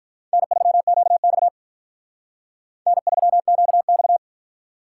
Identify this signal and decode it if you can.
Morse code, heard as a keyed tone, about 43 wpm, 700 Hz. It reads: N3QX N3QX
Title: Speed Racing - USA Call Signs 35wpm